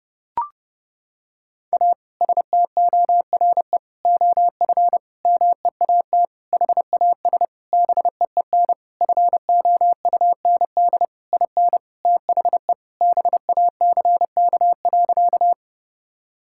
Morse code, heard as a keyed tone, about 31 wpm, 700 Hz. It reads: A STORE OF MEAT HAS BEEN FOUND IN THE BACK.
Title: Sentences from Top 1000 Words - Review Part 5 - 30wpm